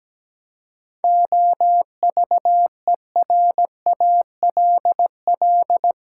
Morse code, heard as a keyed tone, about 17 wpm, 700 Hz. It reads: OVERALL
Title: Seven Letter Words - 17wpm